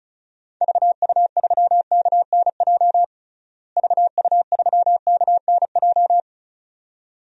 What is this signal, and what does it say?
Morse code, heard as a keyed tone, about 34 wpm, 700 Hz. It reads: VU3KNJ VU3KNJ
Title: DXCC-100 Call Signs - 6 Characters - Repeated 2x - 35wpm